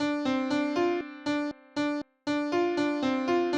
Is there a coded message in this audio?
No.